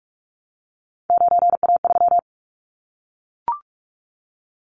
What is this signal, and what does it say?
Morse code, heard as a keyed tone, about 44 wpm, 700 Hz. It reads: 9A3
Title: Callsign Prefixes for Top 25 DXCC - Repeated 3x - 45wpm